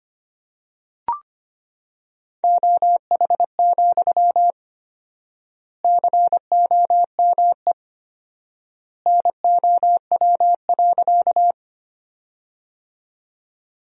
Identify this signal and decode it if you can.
Morse code, heard as a keyed tone, about 25 wpm, 700 Hz. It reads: OH, COME NOW.
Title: 4x Word-Spacing - Sentences from Top 500 Words - 25wpm